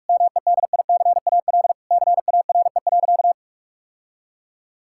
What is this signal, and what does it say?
Morse code, heard as a keyed tone, about 41 wpm, 700 Hz. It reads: MEDICAL CARE.
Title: Sentences from Top 1000 Words - Review Part 5 - 45wpm